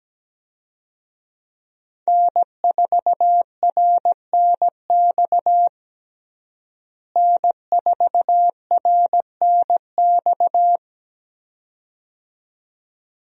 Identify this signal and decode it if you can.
Morse code, heard as a keyed tone, about 17 wpm, 700 Hz. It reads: N4RNX N4RNX
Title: USA Call Signs - 5 Characters - Repeated 2x - 17wpm